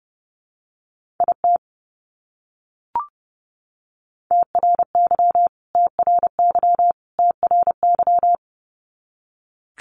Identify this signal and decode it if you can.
Morse code, heard as a keyed tone, about 30 wpm, 700 Hz. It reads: IT TRY TRY TRY
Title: Top 200 Words Repeated 30wpm